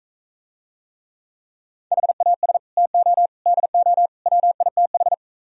Morse code, heard as a keyed tone, about 45 wpm, 700 Hz. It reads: HAS TO DO WITH